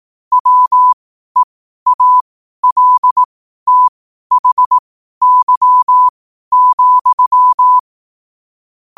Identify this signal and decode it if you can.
Morse code, heard as a keyed tone, about 18 wpm, spaced as for 12 wpm, 1000 Hz. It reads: WEALTHY,
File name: qotdfiles / 12wpm